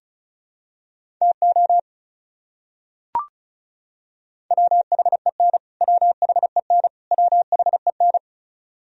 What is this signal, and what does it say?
Morse code, heard as a keyed tone, about 36 wpm, 700 Hz. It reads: TO WHEN WHEN WHEN
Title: Top 100 Words Repeated 35wpm